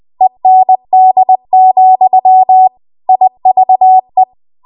Morse code, heard as a keyed tone, about 20 wpm, 750 Hz. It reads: END, IVE
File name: Morse / 20WPM